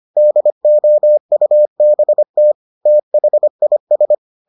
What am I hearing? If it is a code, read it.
Morse code, heard as a keyed tone, about 25 wpm, 600 Hz. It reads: DOUBT THIS